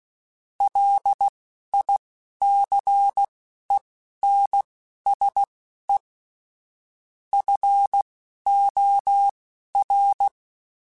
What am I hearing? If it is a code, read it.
Morse code, heard as a keyed tone, about 16 wpm, spaced as for 11 wpm, 800 Hz. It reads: LICENSE FOR